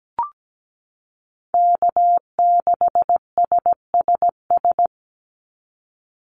Morse code, heard as a keyed tone, about 17 wpm, 700 Hz. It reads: K6SSS
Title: Call Signs - 17wpm